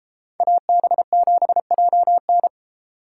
Morse code, heard as a keyed tone, about 29 wpm, 700 Hz. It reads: AB7JN